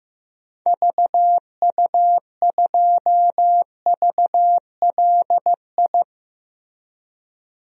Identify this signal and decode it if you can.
Morse code, heard as a keyed tone, about 15 wpm, 700 Hz. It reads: VU2VLI
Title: DXCC-100 Call Signs - 6 Characters - Repeated 2x - 15wpm